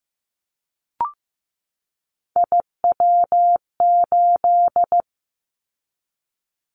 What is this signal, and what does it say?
Morse code, heard as a keyed tone, about 15 wpm, 700 Hz. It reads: IW8
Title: Callsign Prefixes for Top 50 DXCC - 15wpm